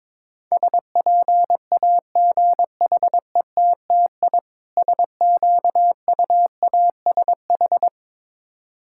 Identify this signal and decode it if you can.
Morse code, heard as a keyed tone, about 22 wpm, 700 Hz. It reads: SPAGHETTI SQUASH
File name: Difficult Words with Hints - 22wpm